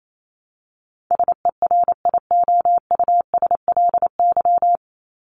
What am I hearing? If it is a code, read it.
Morse code, heard as a keyed tone, about 28 wpm, 700 Hz. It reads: SERIOUSLY
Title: Top 1001-1250 Words 28wpm